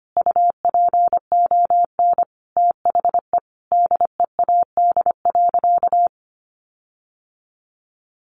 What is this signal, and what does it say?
Morse code, heard as a keyed tone, about 25 wpm, 700 Hz. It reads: UPON THE DEAD.
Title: Sentences from Top 1500 Words 25wpm